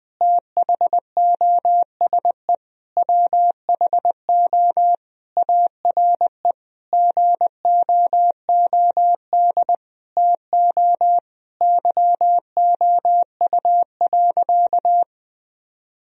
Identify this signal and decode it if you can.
Morse code, heard as a keyed tone, about 20 wpm, 700 Hz. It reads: THOSE WHO ARE GOOD TO YOU.
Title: Sentences from Top 1000 Words - Review Part 1 - 20wpm